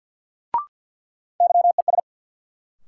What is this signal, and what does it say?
Morse code, heard as a keyed tone, about 53 wpm, 700 Hz. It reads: YES
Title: Sentences from Top 500 Words - New Word Drill - 50wpm